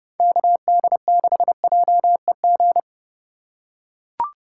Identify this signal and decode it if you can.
Morse code, heard as a keyed tone, about 30 wpm, 700 Hz. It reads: KD6JEG
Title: USA Call Signs - 6 Characters - 3x Character Spacing - 30wpm